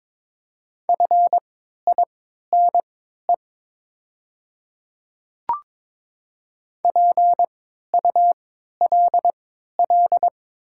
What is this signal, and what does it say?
Morse code, heard as a keyed tone, about 22 wpm, spaced as for 12 wpm, 700 Hz. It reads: FINE PULL
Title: 3x Character Spacing - Top 500 Words - 22wpm